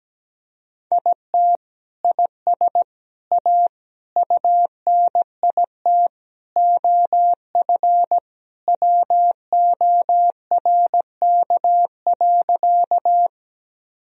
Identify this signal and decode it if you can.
Morse code, heard as a keyed tone, about 17 wpm, 700 Hz. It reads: IT IS A UNIT OF WORK.
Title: Sentences from Top 1000 Words - Review Part 2 - 17wpm